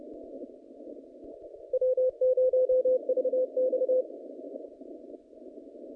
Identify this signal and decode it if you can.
Morse code, heard as a keyed tone, about 30 wpm, 510 Hz. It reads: W0VX
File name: contests / naqp-cw-k5tr-op-k5pi- -0300